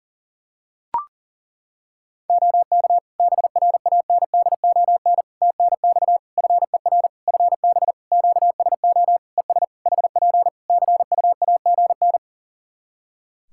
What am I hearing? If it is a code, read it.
Morse code, heard as a keyed tone, about 40 wpm, 700 Hz. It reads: OK BRANDON TNX FER FB QSO ES HP CUAGN
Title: K1USN SST QSO Practice 40wpm